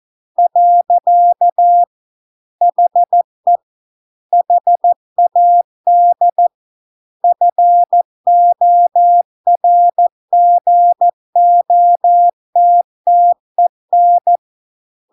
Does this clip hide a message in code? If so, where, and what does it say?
Yes: Morse code from 0.4 to 14.4 s, heard as a keyed tone, about 14 wpm, 700 Hz. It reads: . HE HAD FORGOTTEN